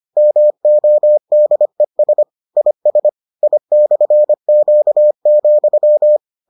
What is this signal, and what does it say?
Morse code, heard as a keyed tone, about 25 wpm, 600 Hz. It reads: MODES IS I/Q,